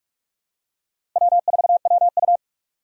Morse code, heard as a keyed tone, about 46 wpm, 700 Hz. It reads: W4WU
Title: USA Call Signs - 4 Characters - Repeated 2x - 45wpm